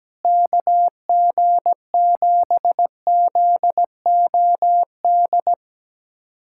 Morse code, heard as a keyed tone, about 17 wpm, 700 Hz. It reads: KG7ZOD
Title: USA Callsign DE Callsign - 6 Characters - 17wpm